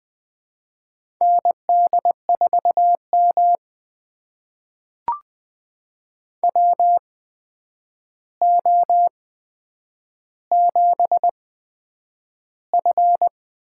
Morse code, heard as a keyed tone, about 20 wpm, spaced as for 9 wpm, 700 Hz. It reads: ND4M WO7F
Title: USA Call Signs - 4 Characters - 8x Character Spacing - 20wpm